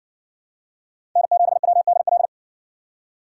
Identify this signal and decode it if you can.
Morse code, heard as a keyed tone, about 56 wpm, 700 Hz. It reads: N7WDL